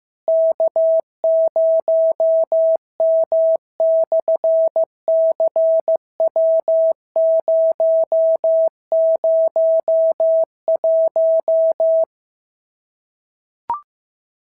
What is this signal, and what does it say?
Morse code, heard as a keyed tone, about 15 wpm, 690 Hz. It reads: K0M/CW001
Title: SOTA QSO Practice 15wpm